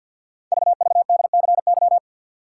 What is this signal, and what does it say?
Morse code, heard as a keyed tone, about 48 wpm, 700 Hz. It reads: VVDCY
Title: Code Groups - 5 Letters - 50wpm